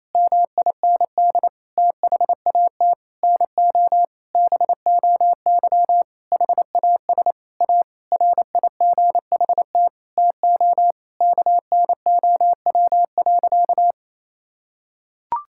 Morse code, heard as a keyed tone, about 27 wpm, 700 Hz. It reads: MIND THAT NO BOY HAS A RIGHT TO KNOW.
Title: Sentences from Top 1000 Words - Review Part 6 - 28wpm